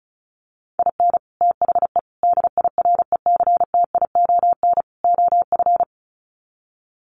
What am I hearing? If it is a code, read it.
Morse code, heard as a keyed tone, about 34 wpm, 700 Hz. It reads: IN THE DIRECTION OF
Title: Sets of 4 Words 35wpm